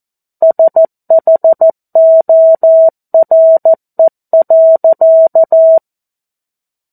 Morse code, heard as a keyed tone, about 14 wpm, 650 Hz. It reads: SHORE.
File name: 02-Chapter-02 14 WPM